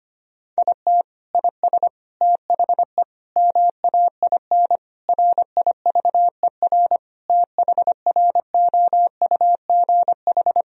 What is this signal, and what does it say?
Morse code, heard as a keyed tone, about 25 wpm, 700 Hz. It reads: IT IS THE MAIN RIVER THROUGH
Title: Sentences from Top 1000 Words - Review Part 2 - 25wpm